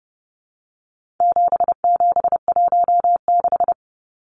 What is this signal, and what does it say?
Morse code, heard as a keyed tone, about 30 wpm, 700 Hz. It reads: 7716
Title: Numbers - 4-Digit 30wpm